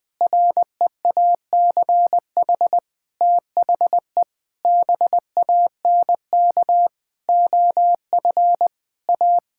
Morse code, heard as a keyed tone, about 20 wpm, 700 Hz. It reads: REACH THE BANK OF A
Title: Sentences from Top 1000 Words - Review Part 6 - 20wpm